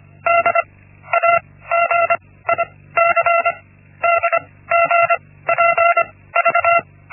Morse code, heard as a keyed tone, about 25 wpm, 680 Hz. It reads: D A G I C D G P V
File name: Linksmakalnis-1